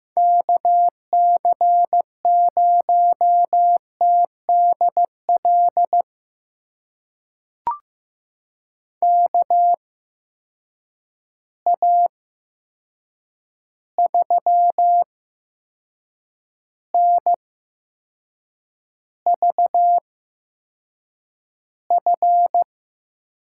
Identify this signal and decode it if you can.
Morse code, heard as a keyed tone, about 15 wpm, spaced as for 6 wpm, 700 Hz. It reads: KC0TDL KA3NVF
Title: USA Call Signs - 6 Characters - 8x Character Spacing - 15wpm